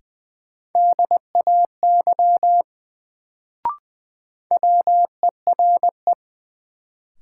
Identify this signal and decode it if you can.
Morse code, heard as a keyed tone, about 20 wpm, 700 Hz. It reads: DAY WERE
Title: Sets of 2 Words - New Word Drill - 20wpm